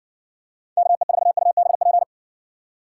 Morse code, heard as a keyed tone, about 60 wpm, 700 Hz. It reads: KE4UBP